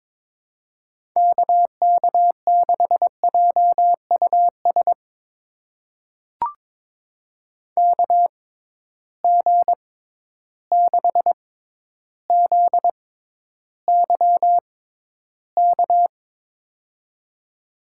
Morse code, heard as a keyed tone, about 22 wpm, spaced as for 11 wpm, 700 Hz. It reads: KK6JUS KG6ZYK
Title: USA Call Signs - 6 Characters - 6x Character Spacing - 22wpm